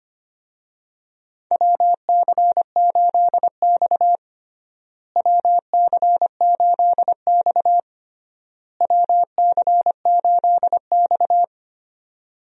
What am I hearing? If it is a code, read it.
Morse code, heard as a keyed tone, about 25 wpm, 700 Hz. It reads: WC8X WC8X WC8X
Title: USA Call Signs - 4 Characters - Repeated 3x - 25wpm